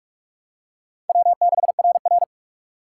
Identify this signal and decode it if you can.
Morse code, heard as a keyed tone, about 48 wpm, 700 Hz. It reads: W6RR
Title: USA Call Signs - 4 Characters - 6x Character Spacing - 45wpm